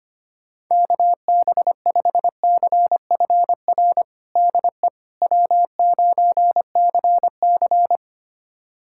Morse code, heard as a keyed tone, about 25 wpm, 700 Hz. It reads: KB5CFR DE W9CC
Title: USA Callsign DE Callsign 25wpm